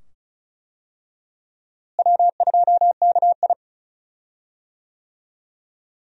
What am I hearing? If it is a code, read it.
Morse code, heard as a keyed tone, about 36 wpm, 700 Hz. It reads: W2KI
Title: USA Call Signs - 4 Characters - Repeated 2x - 35wpm